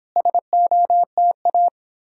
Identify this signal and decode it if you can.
Morse code, heard as a keyed tone, about 31 wpm, 700 Hz. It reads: SOTA